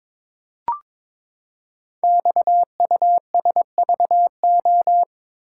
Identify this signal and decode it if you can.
Morse code, heard as a keyed tone, about 22 wpm, 700 Hz. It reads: XUSVO
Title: Code Groups - 5 Letter:Numbers - 22wpm